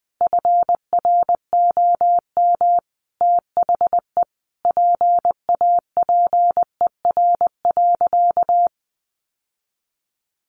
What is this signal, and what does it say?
Morse code, heard as a keyed tone, about 20 wpm, 700 Hz. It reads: FROM THE PAPER.
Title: Sentences from Top 1400 Words 20wpm